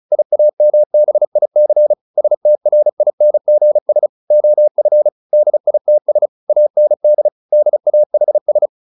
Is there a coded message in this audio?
Yes: Morse code from 0.1 to 8.7 s, heard as a keyed tone, about 35 wpm, 600 Hz. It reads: IAMBIC STRINGS OF DITS AND DAHS